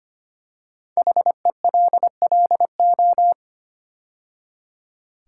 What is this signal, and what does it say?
Morse code, heard as a keyed tone, about 25 wpm, 700 Hz. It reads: HELLO
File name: 5 Letter Words - First 3 Letters Given - 25wpm